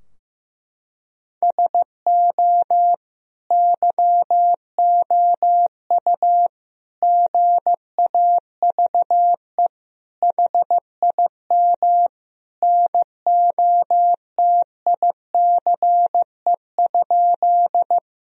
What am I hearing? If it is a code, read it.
Morse code, heard as a keyed tone, about 16 wpm, 700 Hz. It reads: SO YOU GAVE HIM NOTICE?
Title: Sentences from Top 1000 Words - Review Part 2 - 15wpm